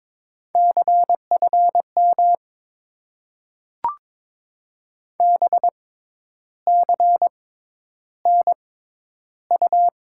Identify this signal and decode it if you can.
Morse code, heard as a keyed tone, about 22 wpm, spaced as for 10 wpm, 700 Hz. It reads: CFM BCNU
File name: Top 90 QSO Elements - 6x Character Spacing - 22wpm